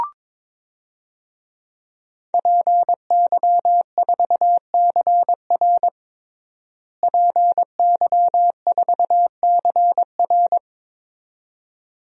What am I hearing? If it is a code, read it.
Morse code, heard as a keyed tone, about 22 wpm, 700 Hz. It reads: PY4CR PY4CR
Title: DXCC-100 Call Signs - 5 Characters - Repeated 2x - 22wpm